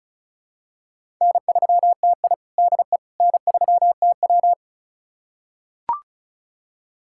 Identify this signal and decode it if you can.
Morse code, heard as a keyed tone, about 36 wpm, 700 Hz. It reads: N3TI DE N3TW
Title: USA Callsign DE Callsign - 4 Characters - 35wpm